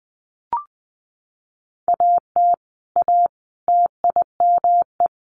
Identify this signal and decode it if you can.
Morse code, heard as a keyed tone, about 20 wpm, 700 Hz. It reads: AT A TIME
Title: Sets of 4 Words 20wpm